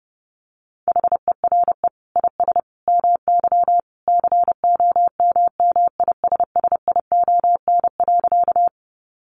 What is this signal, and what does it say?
Morse code, heard as a keyed tone, about 30 wpm, 700 Hz. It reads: HERE IS MY COMMISSION.
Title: Sentences from Top 1400 Words 30wpm